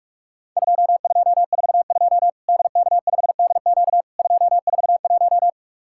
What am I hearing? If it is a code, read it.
Morse code, heard as a keyed tone, about 46 wpm, 700 Hz. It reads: 2242 DK5DQ 241
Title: Worked All Europe DX Contest - QTC Exchange - 45wpm